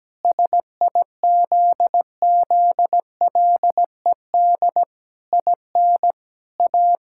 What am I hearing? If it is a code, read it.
Morse code, heard as a keyed tone, about 17 wpm, 700 Hz. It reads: SIZZLED IN A
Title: Pangram Challenge v2 - 17wpm